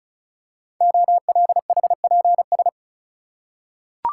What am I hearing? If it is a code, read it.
Morse code, heard as a keyed tone, about 36 wpm, 690 Hz. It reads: OLHPS
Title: Code Groups - 5 Letters - 35wpm